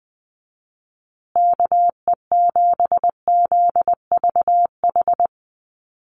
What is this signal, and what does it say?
Morse code, heard as a keyed tone, about 20 wpm, 700 Hz. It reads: KE7ZVH
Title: USA Call Signs Repeated 3x 20wpm